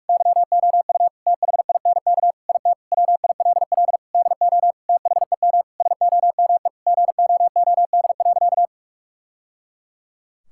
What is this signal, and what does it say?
Morse code, heard as a keyed tone, about 42 wpm, 700 Hz. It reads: YOU THINK IT WILL DO THEM SOME GOOD.
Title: Sentences from Top 1000 Words - Review Part 6 - 45wpm